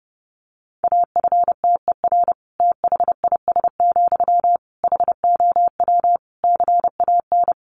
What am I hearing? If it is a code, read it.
Morse code, heard as a keyed tone, about 28 wpm, 700 Hz. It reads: AFTER THIS, HOW CAN
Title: Sentences from Top 1500 Words 30wpm